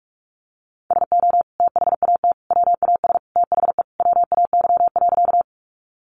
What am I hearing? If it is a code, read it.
Morse code, heard as a keyed tone, about 44 wpm, 700 Hz. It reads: SO THAT WAS THE WAY.
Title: Sentences from Top 100 Words 45wpm